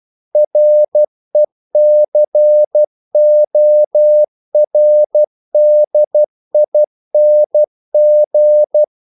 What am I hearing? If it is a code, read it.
Morse code, heard as a keyed tone, about 12 wpm, 600 Hz. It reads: RECORDING
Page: English quotations in CW (telegraphy - morse)